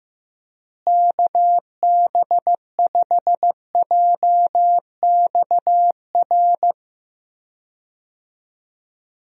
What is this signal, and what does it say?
Morse code, heard as a keyed tone, about 15 wpm, 700 Hz. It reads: KB5JXR
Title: USA Call Signs - 6 Characters - Repeated 3x - 15wpm